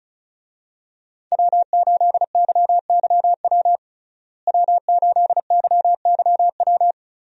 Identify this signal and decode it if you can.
Morse code, heard as a keyed tone, about 34 wpm, 700 Hz. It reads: W8YYW W8YYW
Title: USA Call Signs - 5 Characters - Repeated 3x - 35wpm